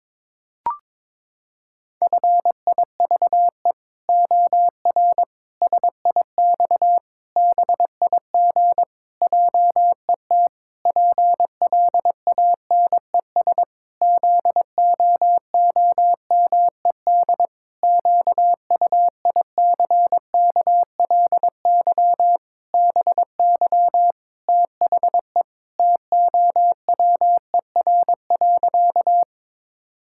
Morse code, heard as a keyed tone, about 21 wpm, 700 Hz. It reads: FIVE OR SIX BIG JET PLANES ZOOMED QUICKLY BY THE TOWER.
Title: Pangram Challenge v2 - 22wpm